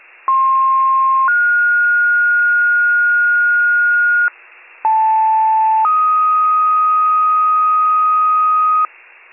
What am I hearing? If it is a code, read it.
Two-tone page, heard as a one-second tone, then a three-second tone, about 1060 then 1430 Hz, about 880 then 1250 Hz, with noise. It is heard from 0.3 to 8.9 s.